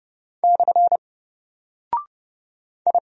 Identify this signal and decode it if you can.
Morse code, heard as a keyed tone, about 29 wpm, 700 Hz. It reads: / I
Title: Single Character (Letter:Number:Punctuation:Prosign) Alt 30wpm